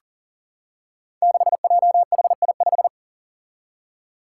Morse code, heard as a keyed tone, about 39 wpm, 700 Hz. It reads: 6JHI5
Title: Five Challenging Letter-Number Combinations 40wpm